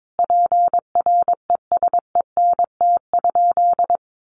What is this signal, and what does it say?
Morse code, heard as a keyed tone, about 22 wpm, 700 Hz. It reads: PRESENT?
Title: Sentences from Top 800 Words 22wpm